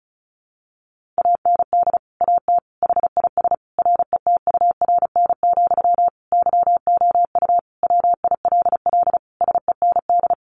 Morse code, heard as a keyed tone, about 34 wpm, 700 Hz. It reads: AND AT HIS RETURN, YOU WILL SEND
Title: Sentences from Top 1500 Words 35wpm